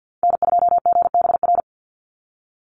Morse code, heard as a keyed tone, about 50 wpm, 700 Hz. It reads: N2GBR
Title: NA SOTA Activator Callsigns 50wpm